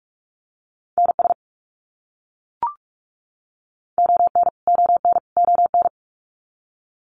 Morse code, heard as a keyed tone, about 44 wpm, 700 Hz. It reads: NS ON ON ON
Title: Canadian Provinces and Territories - Repeated 3x - 45wpm